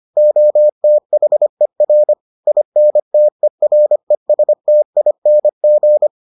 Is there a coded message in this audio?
Yes: Morse code from 0.2 to 6.1 s, heard as a keyed tone, about 25 wpm, 600 Hz. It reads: OTHER INTERESTING